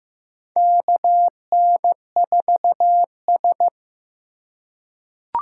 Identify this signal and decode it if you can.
Morse code, heard as a keyed tone, about 15 wpm, 700 Hz. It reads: KN4S
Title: USA Call Signs - 4 Characters - 4x Character Spacing - 15wpm